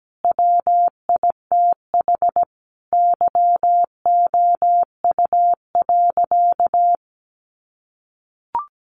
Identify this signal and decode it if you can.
Morse code, heard as a keyed tone, about 17 wpm, 700 Hz. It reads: WITH YOU.
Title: Sentences from Top 500 Words 17wpm